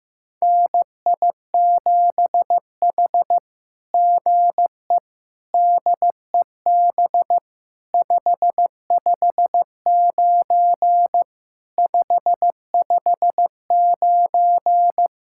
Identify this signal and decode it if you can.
Morse code, heard as a keyed tone, about 15 wpm, 700 Hz. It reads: NI7H GE DEB 559 559
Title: SOTA QSO Practice 15wpm